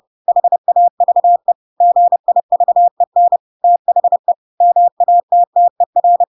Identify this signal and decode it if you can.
Morse code, heard as a keyed tone, about 30 wpm, 700 Hz. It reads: HAVE GIVEN THE MATTER
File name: Pinocchio / Pinocchio ch3 30 30 ik1hge